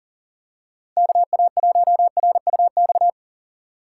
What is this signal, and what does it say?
Morse code, heard as a keyed tone, about 40 wpm, 700 Hz. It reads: KA1RUX
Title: USA Call Signs - 4x Character Spacing - 40wpm